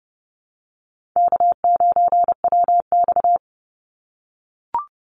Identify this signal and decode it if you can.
Morse code, heard as a keyed tone, about 30 wpm, 700 Hz. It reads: K9WX
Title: Easy Call Signs - 30wpm